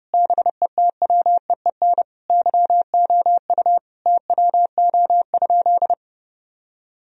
Morse code, heard as a keyed tone, about 32 wpm, 700 Hz. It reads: BETWEEN YOU TWO?
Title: Sentences from Top 1000 Words - Review Part 1 - 30wpm